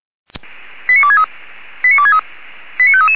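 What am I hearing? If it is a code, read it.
Five-tone selective call (ZVEI), heard as quick runs of five tones, tones of about 70 ms. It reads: 89263 89263 89533
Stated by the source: ZVEI five-tone selective call received by SDR.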